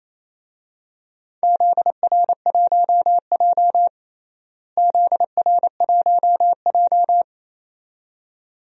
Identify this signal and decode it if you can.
Morse code, heard as a keyed tone, about 28 wpm, 700 Hz. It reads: ZR1J ZR1J
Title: DXCC-100 Call Signs - 4 Characters - Repeated 2x - 28wpm